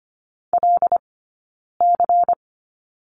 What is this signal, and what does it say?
Morse code, heard as a keyed tone, about 25 wpm, 700 Hz. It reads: L C
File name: course / 25:10wpm / 099 Lesson 17d - Letter u - Two Characters 25wpm